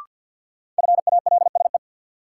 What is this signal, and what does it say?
Morse code, heard as a keyed tone, about 52 wpm, 700 Hz. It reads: FALSE
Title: Sentences from Top 1600 Words - New Word Drill Alt - 50wpm